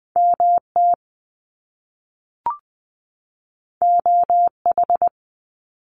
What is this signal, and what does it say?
Morse code, heard as a keyed tone, about 20 wpm, 700 Hz. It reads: MT OH
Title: US States 20wpm